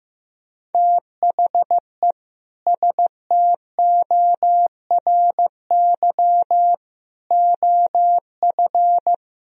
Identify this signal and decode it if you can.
Morse code, heard as a keyed tone, about 15 wpm, 700 Hz. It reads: THE STORY OF
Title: Sentences from Top 1000 Words - Review Part 1 - 15wpm